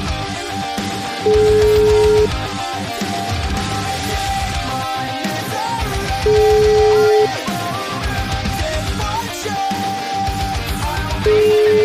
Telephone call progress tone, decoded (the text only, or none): ringback tone